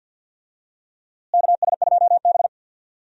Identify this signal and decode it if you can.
Morse code, heard as a keyed tone, about 51 wpm, 700 Hz. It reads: KS2B
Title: USA Call Signs - 4 Characters - Repeated 3x - 50wpm